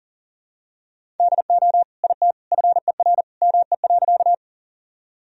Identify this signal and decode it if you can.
Morse code, heard as a keyed tone, about 40 wpm, 700 Hz. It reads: DO IT FER ME.
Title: Sentences from Top 200 Words and Abbreviations 40wpm